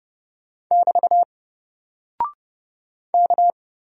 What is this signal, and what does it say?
Morse code, heard as a keyed tone, about 30 wpm, 700 Hz. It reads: <BT> K
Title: Single Character (Letter:Number:Punctuation:Prosign) Alt 30wpm